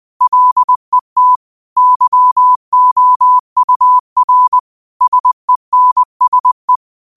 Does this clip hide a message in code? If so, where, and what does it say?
Yes: Morse code from 0.2 to 6.8 s, heard as a keyed tone, about 20 wpm, 1000 Hz. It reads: LET YOUR SENSE